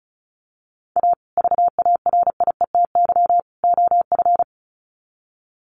Morse code, heard as a keyed tone, about 34 wpm, 700 Hz. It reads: A VARIETY OF
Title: Sets of 3 Words 35wpm